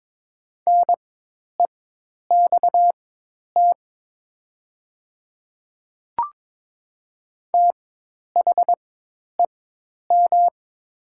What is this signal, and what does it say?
Morse code, heard as a keyed tone, about 22 wpm, spaced as for 10 wpm, 700 Hz. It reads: NEXT THEM
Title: 4x Character Spacing - Top 300 Words - 22wpm